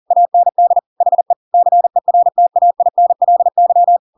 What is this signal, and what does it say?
Morse code, heard as a keyed tone, about 40 wpm, 700 Hz. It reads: AND HE CERTAINLY
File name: Pinocchio / Pinocchio ch5 40 40 ik1hge